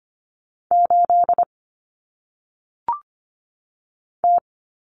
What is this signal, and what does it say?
Morse code, heard as a keyed tone, about 25 wpm, 700 Hz. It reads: 8 T
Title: Common Cut Numbers - 1-Digit 25wpm